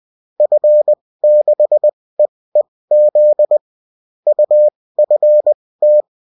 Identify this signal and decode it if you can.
Morse code, heard as a keyed tone, about 20 wpm, spaced as for 16 wpm, 600 Hz. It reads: F6EEZ UFT